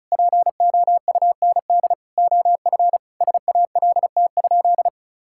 Morse code, heard as a keyed tone, about 36 wpm, 700 Hz. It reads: POUND OF SALT?
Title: Sentences from Top 1000 Words - Review Part 5 - 35wpm